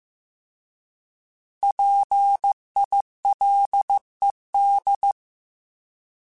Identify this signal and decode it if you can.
Morse code, heard as a keyed tone, about 15 wpm, 800 Hz. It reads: PILED
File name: Alphabetical / Alphabetical mp3s / Ps @13